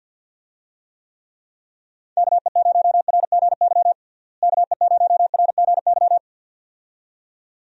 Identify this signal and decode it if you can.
Morse code, heard as a keyed tone, about 52 wpm, 700 Hz. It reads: KE0RGY KE0RGY
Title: USA Call Signs - 6 Characters - Repeated 2x - 50wpm